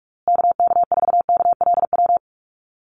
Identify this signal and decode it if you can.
Morse code, heard as a keyed tone, about 44 wpm, 700 Hz. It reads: KK4KLW
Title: USA Call Signs Repeated 3x 45wpm